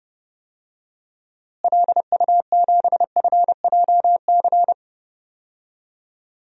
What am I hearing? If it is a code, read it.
Morse code, heard as a keyed tone, about 30 wpm, 700 Hz. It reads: LU7FJC
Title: DXCC-100 Call Signs - 6 Characters - Repeated 3x - 30wpm